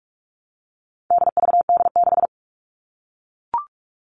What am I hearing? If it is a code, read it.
Morse code, heard as a keyed tone, about 44 wpm, 700 Hz. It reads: DVD6
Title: All Cut Numbers - 4-Digit 45wpm